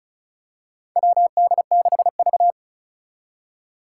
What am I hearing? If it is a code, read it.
Morse code, heard as a keyed tone, about 36 wpm, 700 Hz. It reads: WD6V
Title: Four Challenging Letter-Number Combinations 35wpm